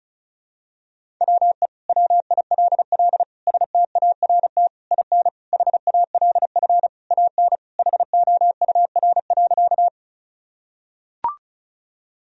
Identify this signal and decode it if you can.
Morse code, heard as a keyed tone, about 36 wpm, 700 Hz. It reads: WE WILL START IN HALF AN HOUR.
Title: Sentences from Top 1000 Words - Review Part 1 - 35wpm